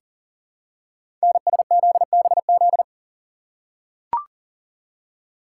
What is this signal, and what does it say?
Morse code, heard as a keyed tone, about 42 wpm, 700 Hz. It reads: NSZBZ
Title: Code Groups - 5 Letters - 40wpm